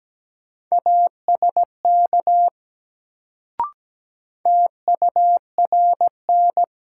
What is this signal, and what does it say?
Morse code, heard as a keyed tone, about 17 wpm, 700 Hz. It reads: ASK TURN
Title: Sets of 4 Words - New Word Drill - 17wpm